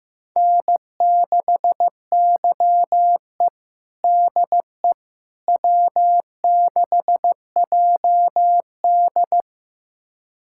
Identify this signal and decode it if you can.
Morse code, heard as a keyed tone, about 15 wpm, 700 Hz. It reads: N6YE DE W6JD
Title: USA Callsign DE Callsign - 4 Characters - 15wpm